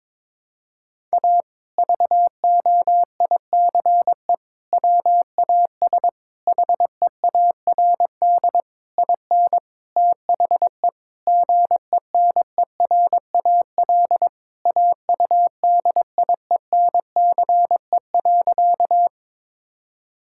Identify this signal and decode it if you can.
Morse code, heard as a keyed tone, about 21 wpm, 700 Hz. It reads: A VOICE WAS HEARD IN THE GENERAL AUDIENCE.
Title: Sentences from Top 1000 Words - Review Part 5 - 22wpm